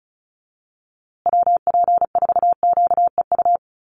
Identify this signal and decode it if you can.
Morse code, heard as a keyed tone, about 34 wpm, 700 Hz. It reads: WP4QEU
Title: USA Call Signs Repeated 3x 35wpm